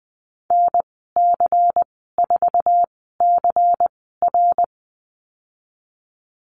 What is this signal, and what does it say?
Morse code, heard as a keyed tone, about 20 wpm, spaced as for 14 wpm, 700 Hz. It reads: NC4CR
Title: 147 Lesson 24g - Letter y - Call Signs 20wpm